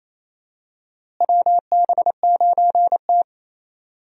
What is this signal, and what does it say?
Morse code, heard as a keyed tone, about 28 wpm, 700 Hz. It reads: WB9T
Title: USA Call Signs - 4 Characters - Repeated 2x - 28wpm